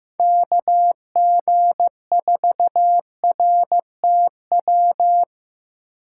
Morse code, heard as a keyed tone, about 15 wpm, 700 Hz. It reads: KG4RTW
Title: USA Call Signs - 6 Characters - 2x Character Spacing - 15wpm